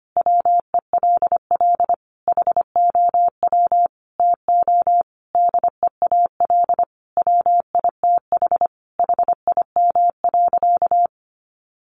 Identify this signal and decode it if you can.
Morse code, heard as a keyed tone, about 25 wpm, 700 Hz. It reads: WELL HOW TO DEAL WITH HIM.
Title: Sentences from Top 800 Words 25wpm